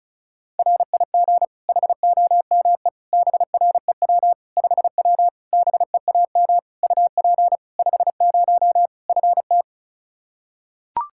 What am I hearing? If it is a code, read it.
Morse code, heard as a keyed tone, about 34 wpm, 700 Hz. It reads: RIG HOME BREW 5W BEAM UP 50 FT
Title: Practice QSO - Exchange 2 - Info - 35wpm